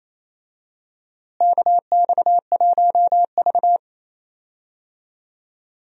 Morse code, heard as a keyed tone, about 27 wpm, 700 Hz. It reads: KX1V
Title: USA Call Signs - 4 Characters - Repeated 3x - 28wpm